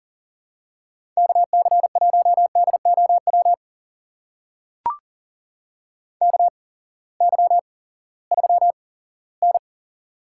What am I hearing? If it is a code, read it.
Morse code, heard as a keyed tone, about 38 wpm, spaced as for 20 wpm, 700 Hz. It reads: KC1DOW KY3N
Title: USA Call Signs - 8x Character Spacing - 40wpm